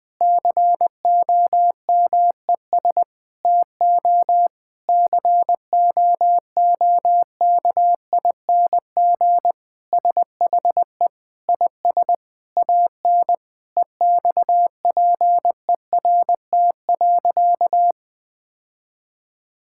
Morse code, heard as a keyed tone, about 20 wpm, 700 Hz. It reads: COMES TO COOKING SHE IS AN EXPERT.
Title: Sentences with 5-Word Phrases - 20wpm